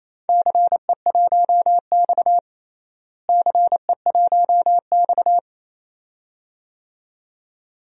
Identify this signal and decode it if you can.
Morse code, heard as a keyed tone, about 28 wpm, 700 Hz. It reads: CE1X CE1X
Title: DXCC-100 Call Signs - 4 Characters - Repeated 2x - 28wpm